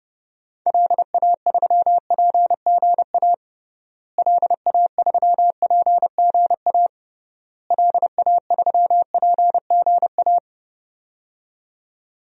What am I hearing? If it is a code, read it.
Morse code, heard as a keyed tone, about 30 wpm, 700 Hz. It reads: LA3PGA LA3PGA LA3PGA
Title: DXCC-100 Call Signs - 6 Characters - Repeated 3x - 30wpm